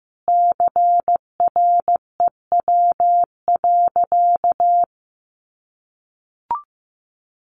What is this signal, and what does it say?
Morse code, heard as a keyed tone, about 15 wpm, 700 Hz. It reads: CREW.
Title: Sentences from Top 1500 Words 15wpm